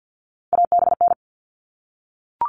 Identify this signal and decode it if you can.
Morse code, heard as a keyed tone, about 48 wpm, 700 Hz. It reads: ABN
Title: All Cut Numbers - 3-Digit 50wpm